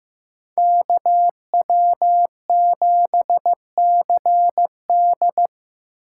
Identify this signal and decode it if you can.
Morse code, heard as a keyed tone, about 17 wpm, 700 Hz. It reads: KW7CD
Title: Speed Racing - USA 2x2 Call Signs 15wpm